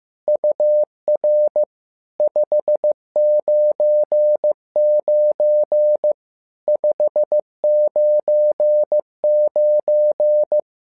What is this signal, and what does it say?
Morse code, heard as a keyed tone, about 15 wpm, 600 Hz. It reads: UR 599 599
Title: SOTA QSO Practice 15wpm